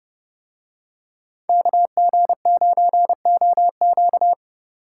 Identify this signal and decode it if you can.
Morse code, heard as a keyed tone, about 28 wpm, 700 Hz. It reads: KG9OQ